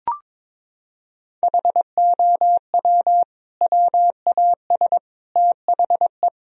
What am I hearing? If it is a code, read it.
Morse code, heard as a keyed tone, about 22 wpm, 700 Hz. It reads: HOW WAS THE
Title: Sentences from Top 1000 Words - Review Part 4 - 22wpm